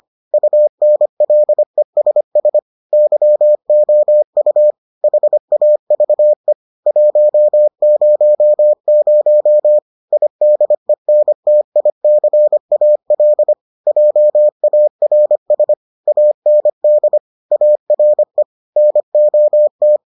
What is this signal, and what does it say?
Morse code, heard as a keyed tone, about 25 wpm, 600 Hz. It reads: UNLESS YOU HAVE 100 IDENTICAL JARS AND ARE NOT